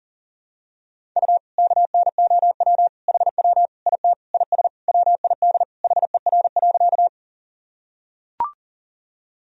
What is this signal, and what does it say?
Morse code, heard as a keyed tone, about 40 wpm, 700 Hz. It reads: U KNOW HW IT IS WID HER.
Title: Sentences from Top 100 Words and Abbreviations 40wpm